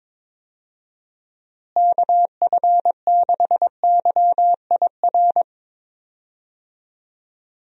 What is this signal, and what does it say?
Morse code, heard as a keyed tone, about 22 wpm, 700 Hz. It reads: KF6YIR
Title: USA Call Signs - 6 Characters - Repeated 2x - 22wpm